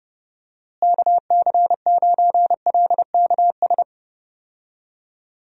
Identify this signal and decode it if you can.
Morse code, heard as a keyed tone, about 30 wpm, 700 Hz. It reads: KC9LKS
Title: USA Call Signs - 6 Characters - Repeated 3x - 30wpm